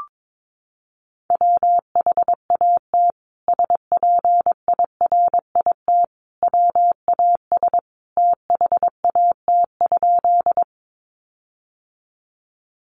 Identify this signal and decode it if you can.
Morse code, heard as a keyed tone, about 22 wpm, 700 Hz. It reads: WHAT SPIRIT WAS THAT?
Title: Sentences from Top 1400 Words 22wpm